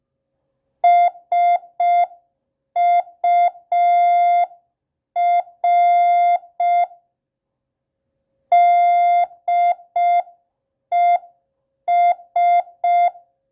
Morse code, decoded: SUR DES